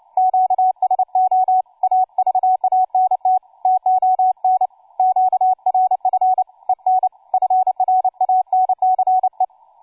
Morse code, heard as a keyed tone, about 29 wpm, 750 Hz, with radio noise. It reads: QSO AVANT TON QRF EN FRANCE
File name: cw forum / tu2 f5ldy 18mhz 16h46